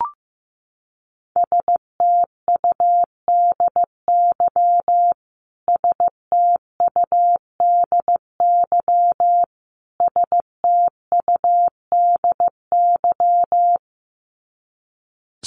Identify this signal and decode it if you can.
Morse code, heard as a keyed tone, about 15 wpm, 700 Hz. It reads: STUDY STUDY STUDY
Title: Top 500 Words Repeated 15wpm